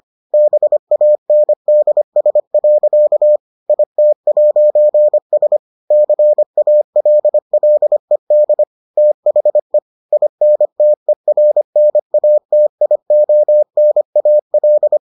Morse code, heard as a keyed tone, about 25 wpm, 600 Hz. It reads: BANDS. IT'S CALLED THE INTERNATIONAL